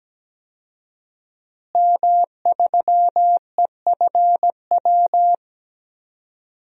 Morse code, heard as a keyed tone, about 17 wpm, 700 Hz. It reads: M3EFW
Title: DXCC-100 Call Signs - 5 Characters - Repeated 2x - 17wpm